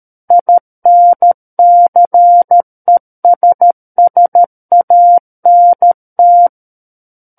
Morse code, heard as a keyed tone, about 13 wpm, 700 Hz. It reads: INCESSANT
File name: Book One / 04-Chapter 4 - 13 WPM